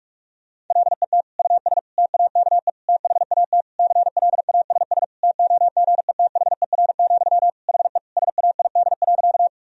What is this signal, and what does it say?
Morse code, heard as a keyed tone, about 46 wpm, 700 Hz. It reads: LET US TAKE THAT CLASS TOGETHER, HE SAID.
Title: Sentences from Top 1000 Words - Review Part 2 - 45wpm